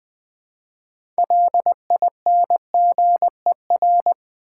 Morse code, heard as a keyed tone, about 20 wpm, 700 Hz. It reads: LINGER
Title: Build A Word - 3 Letters At A Time - 6-Letter Words - 20wpm